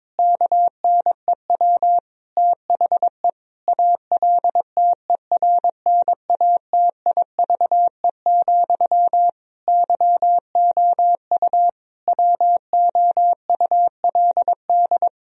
Morse code, heard as a keyed tone, about 22 wpm, 700 Hz. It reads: KNEW THE ALTERNATIVE, YOU WOULD
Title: Sentences from Top 1000 Words - Review Part 4 - 22wpm